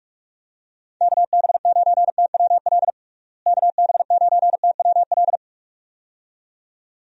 Morse code, heard as a keyed tone, about 48 wpm, 700 Hz. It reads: KB9TWL KB9TWL
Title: USA Call Signs - 6 Characters - Repeated 2x - 45wpm